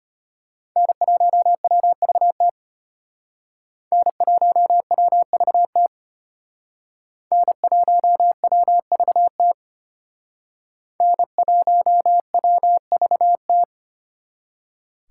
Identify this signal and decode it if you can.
Morse code, heard as a keyed tone, about 32 wpm, 700 Hz. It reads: N1WVT N1WVT N1WVT N1WVT